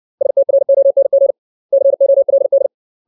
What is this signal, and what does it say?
Morse code, heard as a keyed tone, about 60 wpm, 550 Hz. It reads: STRONG COLD
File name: 632 words / 632 Words 60WPM